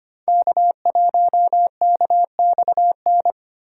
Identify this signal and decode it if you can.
Morse code, heard as a keyed tone, about 29 wpm, 700 Hz. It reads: K1KXN
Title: Speed Racing - USA 1x3 Call Signs 25wpm